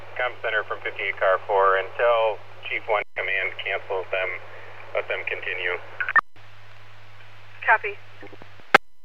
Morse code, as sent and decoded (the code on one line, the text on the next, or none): none